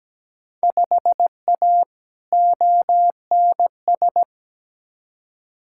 Morse code, heard as a keyed tone, about 17 wpm, 700 Hz. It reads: 5A ONS
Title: Field Day Exchanges Encore - 17wpm